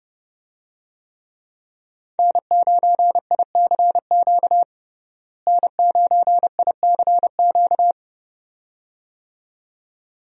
Morse code, heard as a keyed tone, about 32 wpm, 700 Hz. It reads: N9ICQ N9ICQ
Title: USA Call Signs - 5 Characters - Repeated 2x - 30wpm